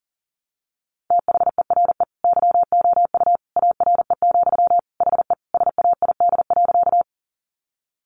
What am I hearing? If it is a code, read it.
Morse code, heard as a keyed tone, about 40 wpm, 700 Hz. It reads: THERE YOU ARE, HE SAID.
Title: Sentences from Top 100 Words 40wpm